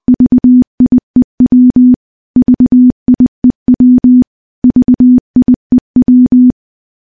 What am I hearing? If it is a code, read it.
Morse code, heard as a keyed tone, about 21 wpm, 250 Hz. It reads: VIEW VIEW VIEW
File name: morse-training-20WPM